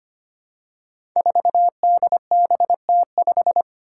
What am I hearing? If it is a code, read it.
Morse code, heard as a keyed tone, about 25 wpm, 700 Hz. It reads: 4DBT5
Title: Five Challenging Letter-Number Combinations 25wpm